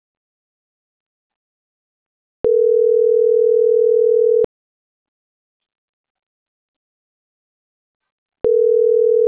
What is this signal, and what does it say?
Ringback tone (North America), 440 + 480 Hz.